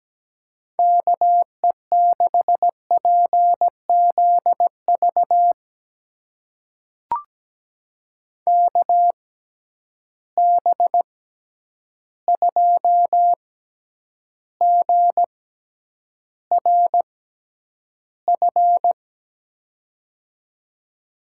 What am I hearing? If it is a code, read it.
Morse code, heard as a keyed tone, about 17 wpm, spaced as for 9 wpm, 700 Hz. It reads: KE6PZV KB2GRF
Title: USA Call Signs - 6 Characters - 6x Character Spacing - 17wpm